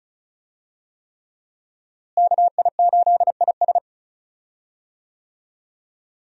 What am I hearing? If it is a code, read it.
Morse code, heard as a keyed tone, about 34 wpm, 700 Hz. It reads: KI8IS
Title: USA Call Signs - 5 Characters - Repeated 3x - 35wpm